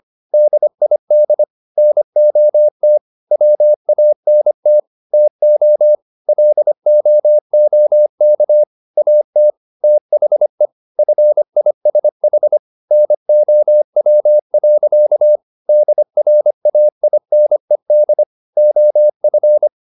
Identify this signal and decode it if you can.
Morse code, heard as a keyed tone, about 25 wpm, 600 Hz. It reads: DID NOT WANT TO LOOK AT THE FISH NOW. DRAINED OF